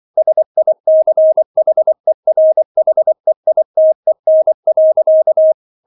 Morse code, heard as a keyed tone, about 24 wpm, 630 Hz. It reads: SICHERHEITEN.